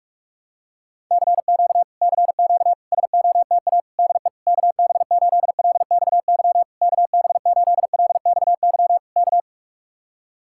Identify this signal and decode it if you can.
Morse code, heard as a keyed tone, about 49 wpm, 700 Hz. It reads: CQ CQ SOTA DE KB8LXY KB8LXY K